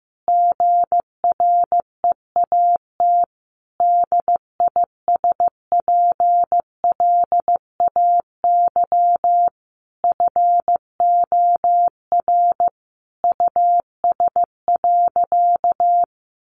Morse code, heard as a keyed tone, about 15 wpm, 700 Hz. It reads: GREAT DISPLAY FOR US.
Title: Sentences from Top 1200 Words 15wpm